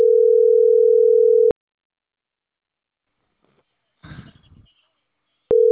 Ringback tone (North America), 440 + 480 Hz.